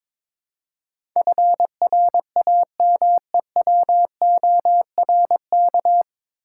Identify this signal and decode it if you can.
Morse code, heard as a keyed tone, about 23 wpm, 700 Hz. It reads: FRAMEWORK
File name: 9 Letter Words - First 3 Letters Given - 22wpm